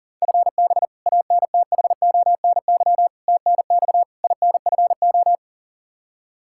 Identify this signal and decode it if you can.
Morse code, heard as a keyed tone, about 40 wpm, 690 Hz. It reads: FB ANTHONY TNX INFO
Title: Practice QSO - Exchange 3 - More Info - 40wpm